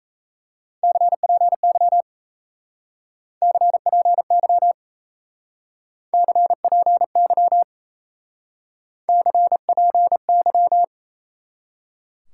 Morse code, heard as a keyed tone, about 37 wpm, 700 Hz. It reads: CPY CPY CPY CPY